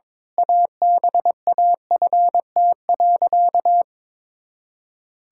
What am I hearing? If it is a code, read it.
Morse code, heard as a keyed tone, about 22 wpm, 700 Hz. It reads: ABAFT.
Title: Pangram Challenge v2 - 22wpm